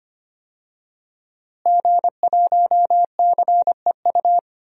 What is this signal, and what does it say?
Morse code, heard as a keyed tone, about 25 wpm, 700 Hz. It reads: G1CEU
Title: DXCC-100 Call Signs - 5 Characters - Repeated 2x - 25wpm